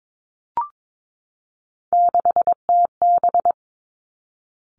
Morse code, heard as a keyed tone, about 22 wpm, 700 Hz. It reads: 6TB
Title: Three Challenging Letter-Number Combinations 22wpm